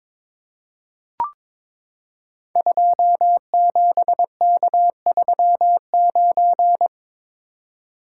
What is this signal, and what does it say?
Morse code, heard as a keyed tone, about 22 wpm, 700 Hz. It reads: 27K39
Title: Code Groups - 5 Letter:Numbers - 22wpm